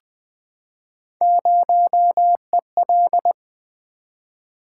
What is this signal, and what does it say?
Morse code, heard as a keyed tone, about 20 wpm, 700 Hz. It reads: 0EL
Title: Three Letter-Number Combinations - 20wpm